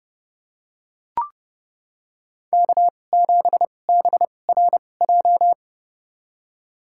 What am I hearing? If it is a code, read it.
Morse code, heard as a keyed tone, about 30 wpm, spaced as for 22 wpm, 700 Hz. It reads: K7BRJ
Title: USA Call Signs - 2x Character Spacing - 30wpm